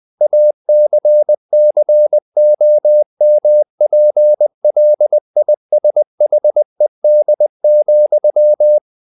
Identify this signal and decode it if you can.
Morse code, heard as a keyed tone, about 20 wpm, 600 Hz. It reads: ACCOMPLISHED,
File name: the old man and the sea part 2 20wpm